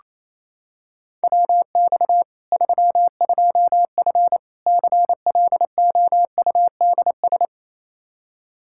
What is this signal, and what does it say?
Morse code, heard as a keyed tone, about 28 wpm, 700 Hz. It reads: WX 32F CLOUDS
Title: Practice QSO - Exchange 2 - Info - 28wpm